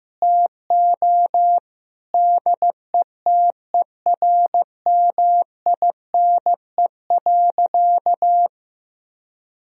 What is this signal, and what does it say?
Morse code, heard as a keyed tone, about 15 wpm, 700 Hz. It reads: TO DETERMINE.
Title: Sentences from Top 1000 Words - Review Part 4 - 15wpm